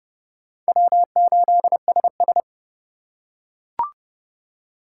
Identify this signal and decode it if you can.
Morse code, heard as a keyed tone, about 30 wpm, 700 Hz. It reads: W8SS
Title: USA Call Signs - 3x Character Spacing - 30wpm